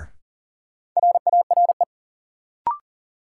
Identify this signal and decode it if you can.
Morse code, heard as a keyed tone, about 40 wpm, 700 Hz. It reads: RARE
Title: Sentences from Top 1100 Words - New Word Drill - 40wpm